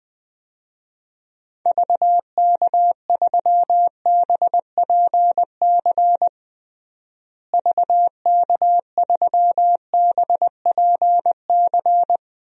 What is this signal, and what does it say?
Morse code, heard as a keyed tone, about 20 wpm, 700 Hz. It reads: VK3BPC VK3BPC
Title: DXCC-100 Call Signs - 6 Characters - Repeated 3x - 20wpm